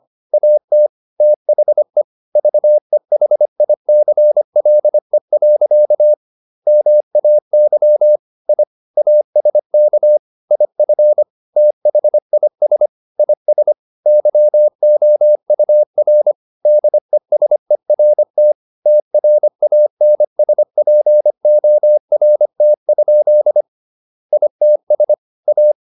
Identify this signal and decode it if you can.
Morse code, heard as a keyed tone, about 25 wpm, 600 Hz. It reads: AT THE VEHICLE. MAY I ASK IF THIS IS YOUR DESERT TRANSPORT? ITS A